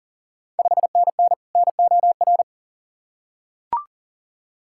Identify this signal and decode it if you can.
Morse code, heard as a keyed tone, about 40 wpm, 690 Hz. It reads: 5NN NOR
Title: Ontario QSO Party Exchange 40wpm